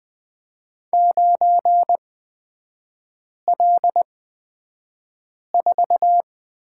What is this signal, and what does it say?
Morse code, heard as a keyed tone, about 20 wpm, spaced as for 5 wpm, 700 Hz. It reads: 9L4
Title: 255 Lesson 41e - Period - Three Characters 20wpm